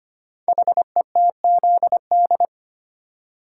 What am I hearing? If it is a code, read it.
Morse code, heard as a keyed tone, about 25 wpm, 690 Hz. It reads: HETZD
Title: Code Groups - 5 Letters - 25wpm